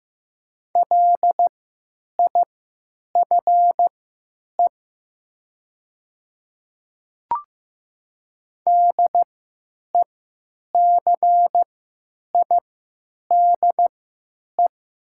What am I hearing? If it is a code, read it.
Morse code, heard as a keyed tone, about 15 wpm, spaced as for 9 wpm, 700 Hz. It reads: LIFE DECIDE
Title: 3x Character Spacing - Top 500 Words - 15wpm